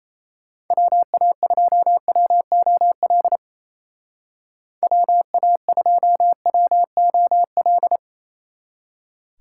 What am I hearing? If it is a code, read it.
Morse code, heard as a keyed tone, about 35 wpm, 700 Hz. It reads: WA2WOL WA2WOL